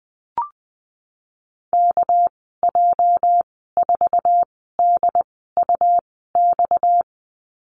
Morse code, heard as a keyed tone, about 20 wpm, spaced as for 14 wpm, 700 Hz. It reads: KJ4DUX
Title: 245 Lesson 39g - Letter z - Call Signs 20wpm